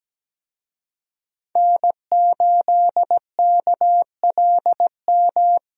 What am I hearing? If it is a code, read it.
Morse code, heard as a keyed tone, about 17 wpm, 700 Hz. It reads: N8KLM
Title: USA Call Signs - 5 Characters - 17wpm